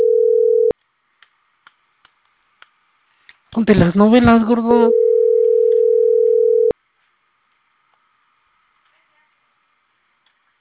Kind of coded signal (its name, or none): telephone call progress tone